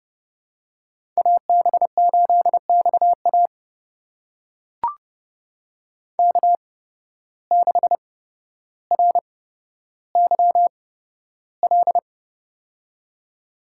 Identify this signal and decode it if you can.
Morse code, heard as a keyed tone, about 32 wpm, spaced as for 13 wpm, 700 Hz. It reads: AB8XA K6RYL
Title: USA Call Signs - 5 Characters - 8x Character Spacing - 30wpm